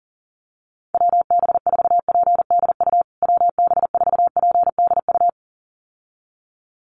Morse code, heard as a keyed tone, about 40 wpm, 700 Hz. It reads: WB4PDU WB4PDU
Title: USA Call Signs Repeated 2x 40wpm